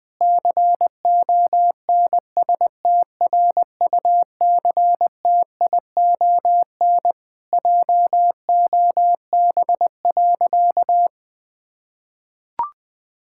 Morse code, heard as a keyed tone, about 20 wpm, 700 Hz. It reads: CONSTRUCTION JOB.
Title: Sentences from Top 1600 Words 20wpm